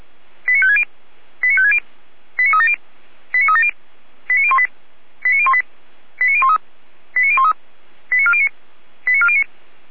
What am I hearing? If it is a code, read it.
Five-tone selective call (ZVEI), heard as quick runs of five tones, tones of about 70 ms. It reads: 89560 89560 89370 89370 89018 89018 89013 89013 89409 89409
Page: /455/SDR/ZVEI/